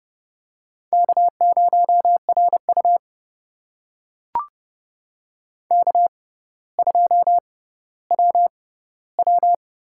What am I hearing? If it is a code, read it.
Morse code, heard as a keyed tone, about 32 wpm, spaced as for 16 wpm, 690 Hz. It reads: K0RU K2WW
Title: USA Call Signs - 4 Characters - 6x Character Spacing - 30wpm